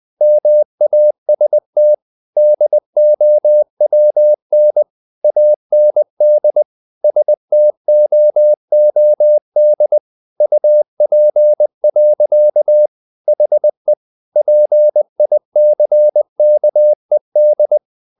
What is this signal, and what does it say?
Morse code, heard as a keyed tone, about 20 wpm, 600 Hz. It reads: MAST DOWN AND STOOD UP. HE PICKED
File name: the old man and the sea part 2 20wpm